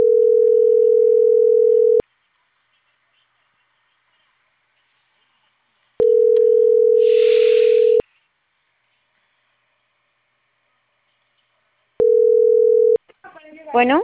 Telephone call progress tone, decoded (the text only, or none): ringback tone